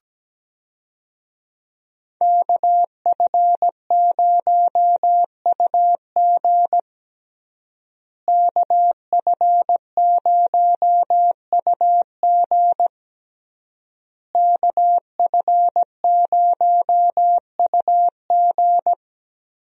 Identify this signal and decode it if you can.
Morse code, heard as a keyed tone, about 17 wpm, 700 Hz. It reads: KF0UG KF0UG KF0UG
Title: USA Call Signs - 5 Characters - Repeated 3x - 17wpm